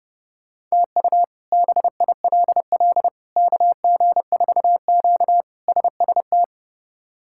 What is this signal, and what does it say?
Morse code, heard as a keyed tone, about 30 wpm, 700 Hz. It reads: TU BILL KG4Q SST
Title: K1USN SST QSO Practice - 30wpm